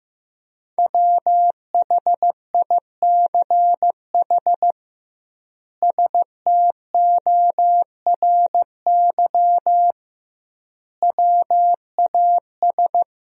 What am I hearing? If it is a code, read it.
Morse code, heard as a keyed tone, about 15 wpm, 700 Hz. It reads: WHICH STORY WAS
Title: 2x Word-Spacing - Sentences from Top 300 Words - 15wpm